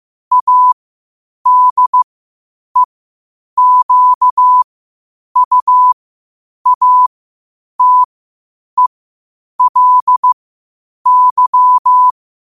Morse code, heard as a keyed tone, about 15 wpm, spaced as for 8 wpm, 1000 Hz. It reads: ADEQUATELY
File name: 8wpm